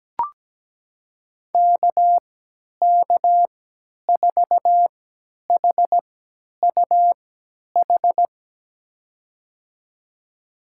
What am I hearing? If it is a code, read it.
Morse code, heard as a keyed tone, about 17 wpm, spaced as for 10 wpm, 700 Hz. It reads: KK4HUH
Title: USA Call Signs - 6 Characters - 3x Character Spacing - 17wpm